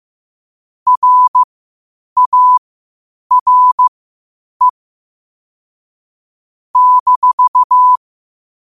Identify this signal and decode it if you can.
Morse code, heard as a keyed tone, about 15 wpm, spaced as for 8 wpm, 1000 Hz. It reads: RARE -
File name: qotdfiles / 8wpm